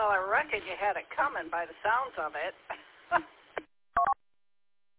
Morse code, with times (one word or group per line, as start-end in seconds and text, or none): none